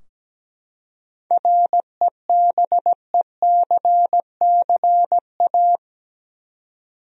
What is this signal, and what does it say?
Morse code, heard as a keyed tone, about 17 wpm, 700 Hz. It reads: REBECCA
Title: USA Common Names - 17wpm